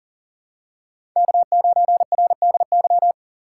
Morse code, heard as a keyed tone, about 40 wpm, 700 Hz. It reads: K9RDY
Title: USA Call Signs - 5 Characters - 3x Character Spacing - 40wpm